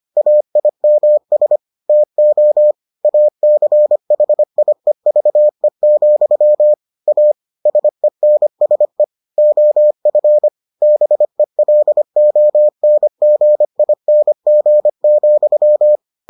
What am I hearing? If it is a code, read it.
Morse code, heard as a keyed tone, about 25 wpm, 600 Hz. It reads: AIMS TO ACHIEVE, A SENSE OF BELONGING,